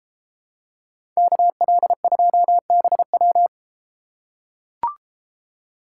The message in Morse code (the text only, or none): KL2BW